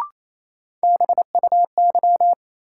Morse code, heard as a keyed tone, about 28 wpm, 700 Hz. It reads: BUY
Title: Sentences from Top 700 Words - New Word Drill - 28wpm